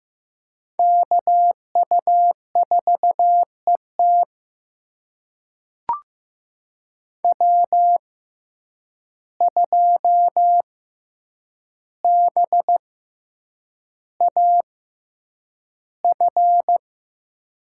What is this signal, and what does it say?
Morse code, heard as a keyed tone, about 15 wpm, spaced as for 8 wpm, 700 Hz. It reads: KU4ET W2BAF
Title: USA Call Signs - 5 Characters - 6x Character Spacing - 15wpm